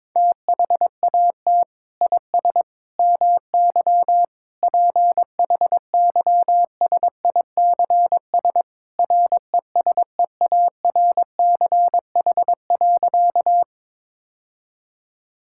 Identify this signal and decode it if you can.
Morse code, heard as a keyed tone, about 22 wpm, 700 Hz. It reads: THAT IS MY PHYSICS RESEARCH.
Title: Sentences from Top 1000 Words - Review Part 5 - 22wpm